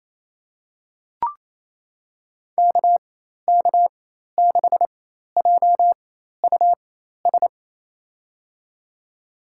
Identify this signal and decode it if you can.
Morse code, heard as a keyed tone, about 27 wpm, spaced as for 13 wpm, 700 Hz. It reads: KK6JUS
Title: USA Call Signs - 6 Characters - 4x Character Spacing - 28wpm